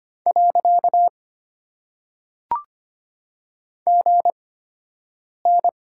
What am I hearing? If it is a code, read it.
Morse code, heard as a keyed tone, about 25 wpm, spaced as for 7 wpm, 700 Hz. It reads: . GN
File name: Top 90 QSO Elements - 8x Character Spacing - 25wpm